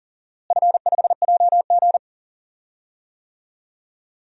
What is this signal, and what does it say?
Morse code, heard as a keyed tone, about 39 wpm, 700 Hz. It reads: F5JG
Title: DXCC-100 Call Signs - 4 Characters - Repeated 2x - 40wpm